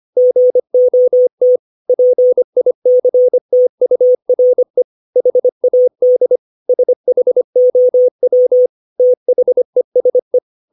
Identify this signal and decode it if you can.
Morse code, heard as a keyed tone, about 25 wpm, 500 Hz. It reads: GOT PICTURE HAD SHOW THESE
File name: 632 words / 632 Words 25WPM